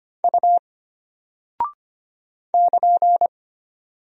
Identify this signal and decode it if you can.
Morse code, heard as a keyed tone, about 25 wpm, 700 Hz. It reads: U <KN>
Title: Single Character (Letter:Number:Punctuation:Prosign) Alt 25wpm